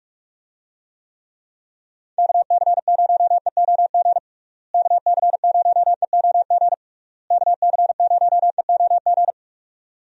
Morse code, heard as a keyed tone, about 42 wpm, 700 Hz. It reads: KC0EOG KC0EOG KC0EOG
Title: USA Call Signs - 6 Characters - Repeated 3x - 45wpm